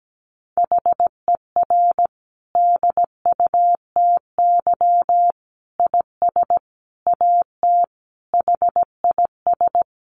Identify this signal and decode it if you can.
Morse code, heard as a keyed tone, about 17 wpm, 700 Hz. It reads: HER DUTY IS AT HIS
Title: Sentences from Top 1500 Words 17wpm